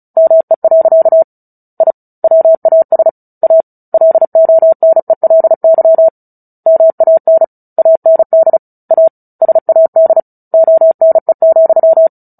Morse code, heard as a keyed tone, about 34 wpm, 650 Hz. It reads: ME. I WAS A LONELY MAN AND A SAD ONE,
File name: Book Two / 09-Chapter 9 - 35 WPM